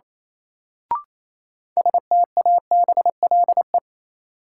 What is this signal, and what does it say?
Morse code, heard as a keyed tone, about 28 wpm, 700 Hz. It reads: STABLE
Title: Sentences from Top 1500 Words - New Word Drill - 28wpm